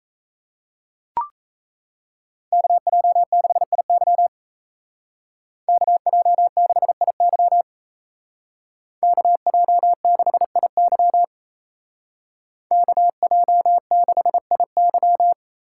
Morse code, heard as a keyed tone, about 33 wpm, 700 Hz. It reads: KJ6IY KJ6IY KJ6IY KJ6IY
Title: Speed Racing - USA 2x2 Call Signs 28wpm